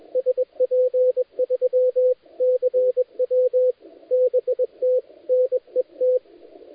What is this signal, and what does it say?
Morse code, heard as a keyed tone, about 20 wpm, 510 Hz, with radio noise. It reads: SP3CW BT NET
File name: dlazaawansowanych / nagraniazpasma / titawaNETcontrol